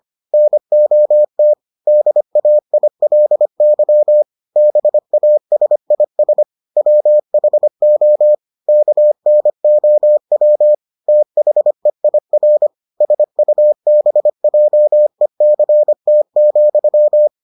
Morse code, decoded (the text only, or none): NOT DAILY BASIS WHO KNOW THEIR SUBJECT,